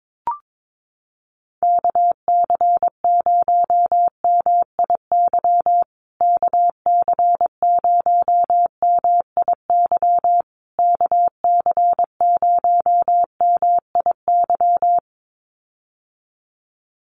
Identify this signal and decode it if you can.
Morse code, heard as a keyed tone, about 22 wpm, 700 Hz. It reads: KC0MIY KC0MIY KC0MIY
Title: USA Call Signs Repeated 3x 22wpm